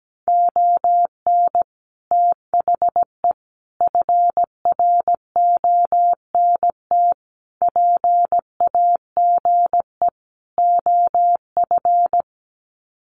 Morse code, heard as a keyed tone, about 17 wpm, 700 Hz. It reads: ON THE FRONT PAGE OF
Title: Sets of 5 Words Encore 17wpm